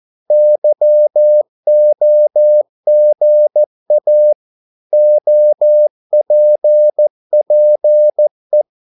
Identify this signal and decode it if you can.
Morse code, heard as a keyed tone, about 14 wpm, 600 Hz. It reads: YOGA OPPE
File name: Korte ord DK 14wpm